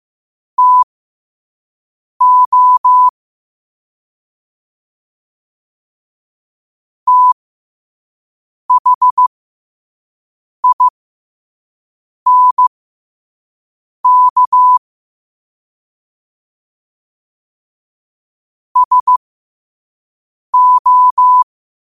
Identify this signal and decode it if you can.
Morse code, heard as a keyed tone, about 15 wpm, spaced as for 5 wpm, 1000 Hz. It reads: TO THINK SO
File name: qotdfiles / 5wpm